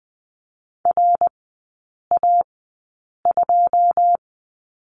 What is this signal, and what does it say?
Morse code, heard as a keyed tone, about 20 wpm, spaced as for 8 wpm, 690 Hz. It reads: RA2
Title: 145 Lesson 24e - Letter y - Three Characters 20wpm